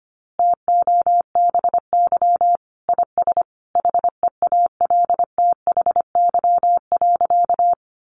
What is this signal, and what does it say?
Morse code, heard as a keyed tone, about 25 wpm, 700 Hz. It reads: TOBY IS HEALTHY.
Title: Easy Phrases 25wpm